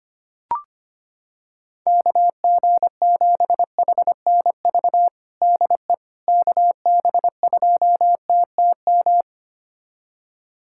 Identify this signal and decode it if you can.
Morse code, heard as a keyed tone, about 25 wpm, 700 Hz. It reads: KG7HNV DE KB2TTM
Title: USA Callsign DE Callsign - 6 Characters - 25wpm